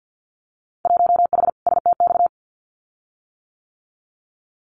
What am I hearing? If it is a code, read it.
Morse code, heard as a keyed tone, about 50 wpm, 700 Hz. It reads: 1H STX
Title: Winter Field Day Exchanges 50wpm